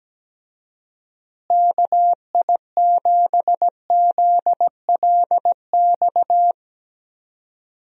Morse code, heard as a keyed tone, about 17 wpm, 700 Hz. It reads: KI7ZLX
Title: USA Call Signs - 6 Characters - Repeated 2x - 17wpm